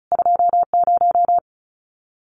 Morse code, heard as a keyed tone, about 36 wpm, 700 Hz. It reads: 20
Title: CQ WW DX Contest Exchange Encore - 35wpm